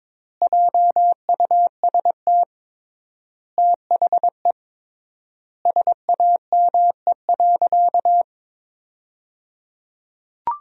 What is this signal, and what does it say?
Morse code, heard as a keyed tone, about 22 wpm, 700 Hz. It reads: JUST THE SAME.
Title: 3x Word-Spacing - Sentences from Top 300 Words - 22wpm